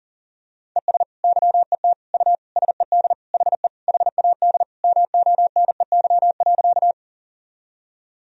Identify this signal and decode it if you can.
Morse code, heard as a keyed tone, about 40 wpm, 700 Hz. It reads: ES YET U SED HE HAD MONEY.
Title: Sentences from Top 500 Words and Abbreviations 40wpm